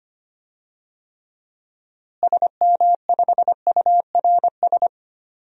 Morse code, heard as a keyed tone, about 25 wpm, 700 Hz. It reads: SM5URS
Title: DXCC-100 Call Signs - 6 Characters - Repeated 3x - 25wpm